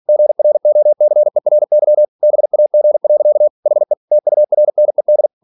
Morse code, heard as a keyed tone, about 46 wpm, 600 Hz. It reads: GROCERY BAG. HE TURNED
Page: Never 47wpm